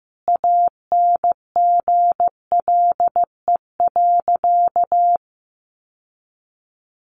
Morse code, heard as a keyed tone, about 15 wpm, 700 Hz. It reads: ANGLE.
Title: Sentences from Top 1500 Words 15wpm